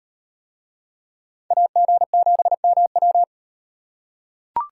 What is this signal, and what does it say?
Morse code, heard as a keyed tone, about 38 wpm, 700 Hz. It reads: AG7MW